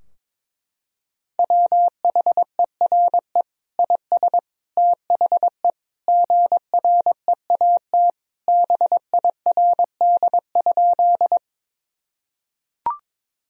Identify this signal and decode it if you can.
Morse code, heard as a keyed tone, about 22 wpm, 700 Hz. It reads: WHERE IS THE GREAT BIRD?
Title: Sentences from Top 1000 Words - Review Part 2 - 22wpm